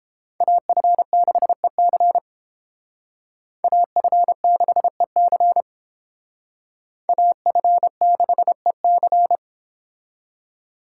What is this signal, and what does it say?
Morse code, heard as a keyed tone, about 32 wpm, 700 Hz. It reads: AF6EC AF6EC AF6EC